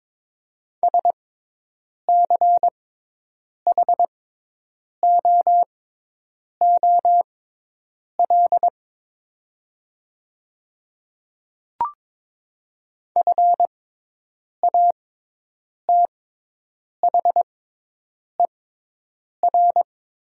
Morse code, heard as a keyed tone, about 22 wpm, spaced as for 8 wpm, 700 Hz. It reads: SCHOOL FATHER
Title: 6x Character Spacing - Top 300 Words - 22wpm